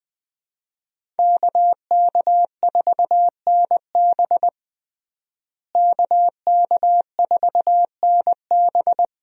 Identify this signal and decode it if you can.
Morse code, heard as a keyed tone, about 20 wpm, 700 Hz. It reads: KK4NB KK4NB
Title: USA Call Signs - 5 Characters - Repeated 3x - 20wpm